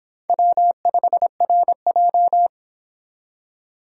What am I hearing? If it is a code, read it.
Morse code, heard as a keyed tone, about 24 wpm, 700 Hz. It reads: W5RJ